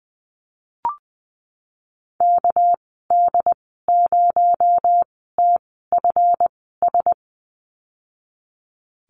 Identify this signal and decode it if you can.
Morse code, heard as a keyed tone, about 20 wpm, spaced as for 15 wpm, 700 Hz. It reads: KD0TFS
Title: 245 Lesson 39g - Letter z - Call Signs 20wpm